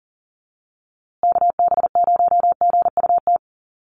Morse code, heard as a keyed tone, about 40 wpm, 700 Hz. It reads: KB0GUT
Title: USA Call Signs Repeated 2x 40wpm